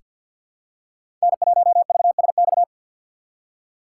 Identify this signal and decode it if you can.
Morse code, heard as a keyed tone, about 49 wpm, 700 Hz. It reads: N1VSX
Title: USA Call Signs - 5 Characters - 6x Character Spacing - 50wpm